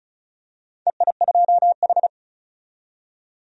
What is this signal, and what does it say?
Morse code, heard as a keyed tone, about 34 wpm, 700 Hz. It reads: EI2H
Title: DXCC-100 Call Signs - 4 Characters - 35wpm